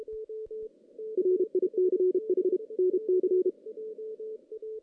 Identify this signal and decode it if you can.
Morse code, heard as a keyed tone, about 32 wpm, 370 Hz, with radio noise. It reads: RICH NC
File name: 2016 JAN CW NAQP / 2016 Jan naqp k5ot - 0-5